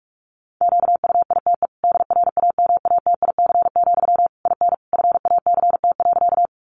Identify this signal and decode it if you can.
Morse code, heard as a keyed tone, about 44 wpm, 700 Hz. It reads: QUITE DRAMATIC, IN FACT.
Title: Sentences from Top 1300 Words 45wpm